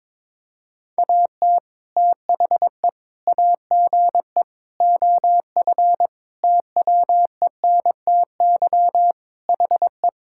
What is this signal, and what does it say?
Morse code, heard as a keyed tone, about 22 wpm, 700 Hz. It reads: AT THE AGE OF TWENTY HE
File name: practice / 22wpm / Sentences with 4-Word Phrases - 22wpm